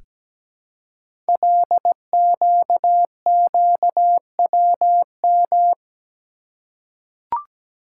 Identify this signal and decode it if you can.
Morse code, heard as a keyed tone, about 17 wpm, 700 Hz. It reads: LQQWM
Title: Code Groups - 5 Letters - 17wpm